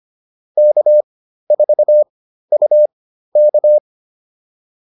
Morse code, heard as a keyed tone, about 25 wpm, spaced as for 13 wpm, 600 Hz. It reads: K4UK
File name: QSO208 13